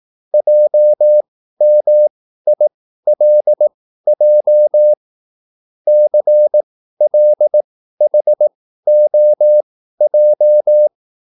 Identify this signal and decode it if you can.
Morse code, heard as a keyed tone, about 18 wpm, spaced as for 13 wpm, 600 Hz. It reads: JMILJ CLHOJ
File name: QSL / CW lesson